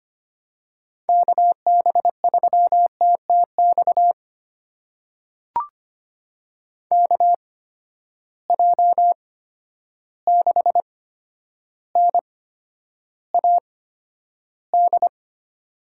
Morse code, heard as a keyed tone, about 25 wpm, spaced as for 11 wpm, 700 Hz. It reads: KB3TTX KJ6NAD
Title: USA Call Signs - 6 Characters - 8x Character Spacing - 25wpm